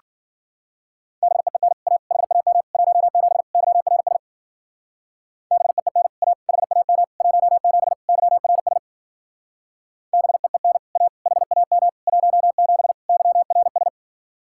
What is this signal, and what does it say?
Morse code, heard as a keyed tone, about 50 wpm, 700 Hz. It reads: BEEN A HAM 17 YRS BEEN A HAM 17 YRS BEEN A HAM 17 YRS
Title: Speed-Racing Common QSO Phrases - 40wpm